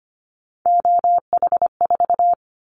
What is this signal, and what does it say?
Morse code, heard as a keyed tone, about 25 wpm, 700 Hz. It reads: OH4
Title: Callsign Prefixes for Top 50 DXCC - 25wpm